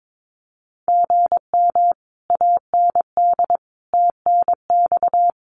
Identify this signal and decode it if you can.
Morse code, heard as a keyed tone, about 22 wpm, 700 Hz. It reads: GM AND TNX
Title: Common QSO Phrases Encore 22wpm